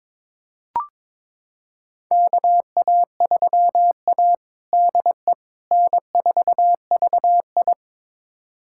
Morse code, heard as a keyed tone, about 22 wpm, 700 Hz. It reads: KA3A DE N4VI
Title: USA Callsign DE Callsign - 4 Characters - 22wpm